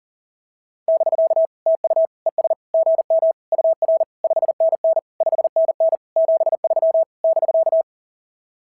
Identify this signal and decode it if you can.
Morse code, heard as a keyed tone, about 44 wpm, 690 Hz. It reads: <BK> TU ES GM UR 5NN 5NN 73 <BK>
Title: SOTA QSO Practice 40wpm